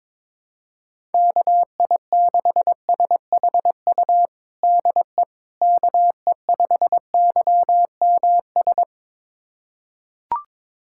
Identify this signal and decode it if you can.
Morse code, heard as a keyed tone, about 22 wpm, 700 Hz. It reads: KI6SHU DE KE5YMS
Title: USA Callsign DE Callsign - 6 Characters - 22wpm